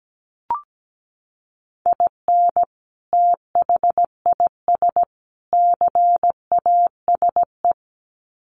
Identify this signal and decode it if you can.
Morse code, heard as a keyed tone, about 17 wpm, 700 Hz. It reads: IN THIS CASE
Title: Sets of 3 Words 17wpm